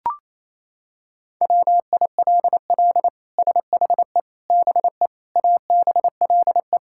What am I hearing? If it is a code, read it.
Morse code, heard as a keyed tone, about 28 wpm, 700 Hz. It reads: WILL SHE BE ABLE
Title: Sentences from Top 1000 Words - Review Part 5 - 28wpm